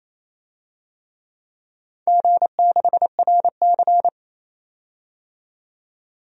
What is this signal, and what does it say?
Morse code, heard as a keyed tone, about 28 wpm, 700 Hz. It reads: G6RC
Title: DXCC-100 Call Signs - 4 Characters - 28wpm